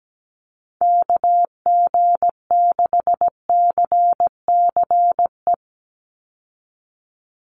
Morse code, heard as a keyed tone, about 17 wpm, 700 Hz. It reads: KG6CCE
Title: USA Call Signs 17wpm